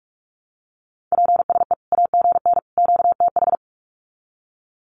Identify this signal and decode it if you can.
Morse code, heard as a keyed tone, about 45 wpm, 700 Hz. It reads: PSE AGN QTH
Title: Common QSO Phrases 45wpm